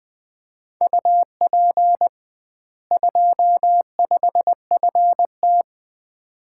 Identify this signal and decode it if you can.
Morse code, heard as a keyed tone, about 21 wpm, 700 Hz. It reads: UP 25FT
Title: Common QSO Phrases - 2x Word-Spacing - 20wpm